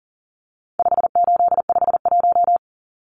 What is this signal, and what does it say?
Morse code, heard as a keyed tone, about 37 wpm, 700 Hz. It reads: 5851
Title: Numbers - 4-Digit 40wpm